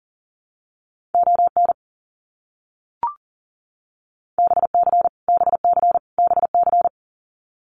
Morse code, heard as a keyed tone, about 38 wpm, 700 Hz. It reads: ON BC BC BC
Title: Canadian Provinces and Territories - Repeated 3x - 40wpm